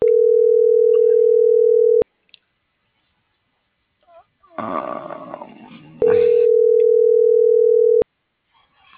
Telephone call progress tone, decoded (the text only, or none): ringback tone